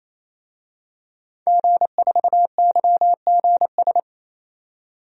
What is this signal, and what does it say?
Morse code, heard as a keyed tone, about 28 wpm, 700 Hz. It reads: G4YGS
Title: DXCC-100 Call Signs - 5 Characters - 28wpm